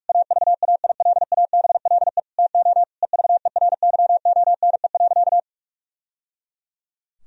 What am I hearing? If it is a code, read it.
Morse code, heard as a keyed tone, about 46 wpm, 700 Hz. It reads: AVAILABLE TO EVERYONE.
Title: Sentences from Top 1000 Words - Review Part 3 - 45wpm